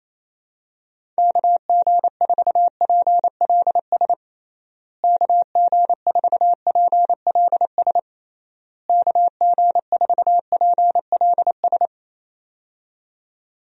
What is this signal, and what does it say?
Morse code, heard as a keyed tone, about 28 wpm, 700 Hz. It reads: KG4PLS KG4PLS KG4PLS
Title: USA Call Signs - 6 Characters - Repeated 3x - 28wpm